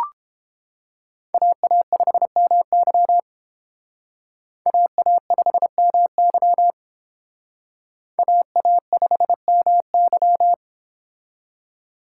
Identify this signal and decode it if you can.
Morse code, heard as a keyed tone, about 29 wpm, 700 Hz. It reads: AA5MY AA5MY AA5MY